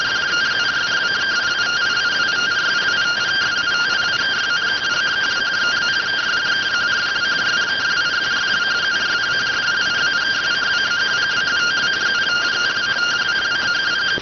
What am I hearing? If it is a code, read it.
Two-tone teleprinter signal, about 170 Hz shift.